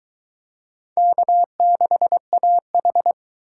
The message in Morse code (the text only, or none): K6AH